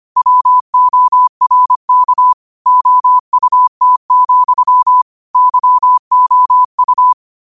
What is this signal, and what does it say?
Morse code, heard as a keyed tone, about 25 wpm, 1000 Hz. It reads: WORK OUT, YOU
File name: qotdfiles / 25wpm